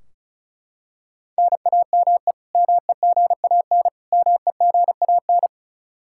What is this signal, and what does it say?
Morse code, heard as a keyed tone, about 34 wpm, 700 Hz. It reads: NAME MEGAN MEGAN
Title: Practice QSO - Exchange 1 - Beginning - 35wpm